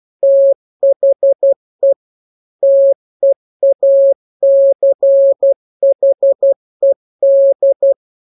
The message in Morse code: THE TEACHED